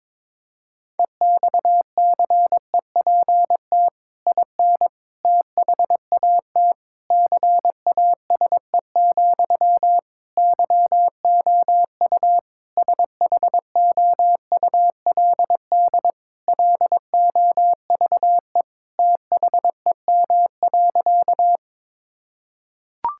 Morse code, heard as a keyed tone, about 22 wpm, 700 Hz. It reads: EXCEPT IN THAT CASE, YOU SHOULD LOVE THEM.
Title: Sentences from Top 1600 Words 22wpm